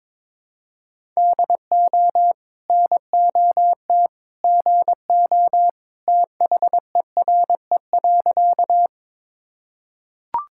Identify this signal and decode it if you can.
Morse code, heard as a keyed tone, about 22 wpm, 700 Hz. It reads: DO NOT GO THERE.
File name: practice / 22wpm / Sentences with 2-Word Phrases - 22wpm